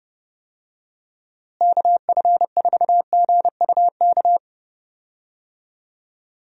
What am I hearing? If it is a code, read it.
Morse code, heard as a keyed tone, about 30 wpm, 700 Hz. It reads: KF4GUK
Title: USA Call Signs - 6 Characters - Repeated 2x - 30wpm